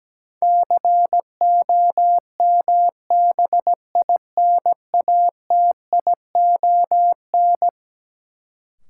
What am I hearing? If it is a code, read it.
Morse code, heard as a keyed tone, about 17 wpm, 700 Hz. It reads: COMBINATION
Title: Sentences from Top 1300 Words - New Word Drill Alt - 17wpm